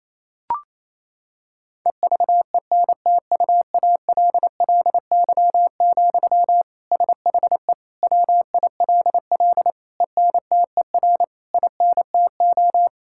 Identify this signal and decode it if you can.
Morse code, heard as a keyed tone, about 28 wpm, 700 Hz. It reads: EVENTUALLY, SHE WILL ENTER INTO
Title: Sentences from Top 1000 Words - Review Part 6 - 28wpm